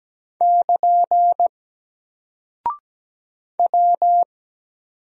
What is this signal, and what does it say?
Morse code, heard as a keyed tone, about 17 wpm, 700 Hz. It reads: <KN> W
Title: Single Character (Letter:Number:Punctuation:Prosign) Alt 17wpm